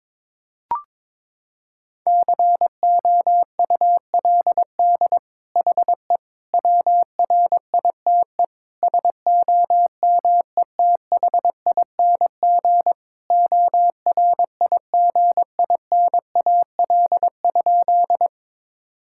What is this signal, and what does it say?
Morse code, heard as a keyed tone, about 22 wpm, 700 Hz. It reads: COULD HE WRITE SOMETHING ORIGINAL?
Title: Sentences from Top 1000 Words - Review Part 6 - 22wpm